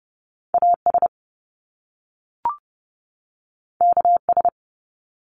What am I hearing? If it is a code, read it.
Morse code, heard as a keyed tone, about 30 wpm, 700 Hz. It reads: AS KS
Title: Top Two Letter Combinations in Words 30wpm